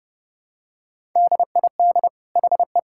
Morse code, heard as a keyed tone, about 32 wpm, 700 Hz. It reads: DID HE
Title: Sentences from Top 1000 Words - Review Part 6 - 30wpm